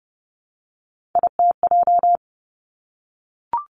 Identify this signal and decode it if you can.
Morse code, heard as a keyed tone, about 30 wpm, 700 Hz. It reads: ITJ
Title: Three Challenging Letter-Number Combinations 30wpm